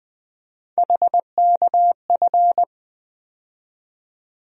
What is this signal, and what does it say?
Morse code, heard as a keyed tone, about 21 wpm, 700 Hz. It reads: HKF
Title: USA Call Sign Postfixes - 3 Letters 20wpm